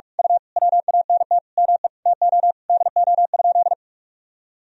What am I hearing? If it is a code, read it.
Morse code, heard as a keyed tone, about 41 wpm, 700 Hz. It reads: U WANT ME TO DO?
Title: Sentences from Top 200 Words and Abbreviations 45wpm